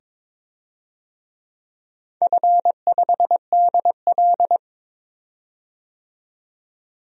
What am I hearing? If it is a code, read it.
Morse code, heard as a keyed tone, about 23 wpm, 700 Hz. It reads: F5DL
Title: DXCC-100 Call Signs - 4 Characters - 22wpm